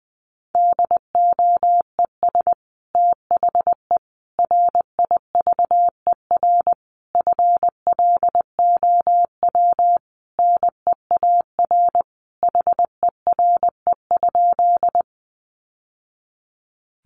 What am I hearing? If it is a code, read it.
Morse code, heard as a keyed tone, about 20 wpm, 700 Hz. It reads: DOES THE RIVER FLOW NEAR HERE?
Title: Sentences from Top 1200 Words 20wpm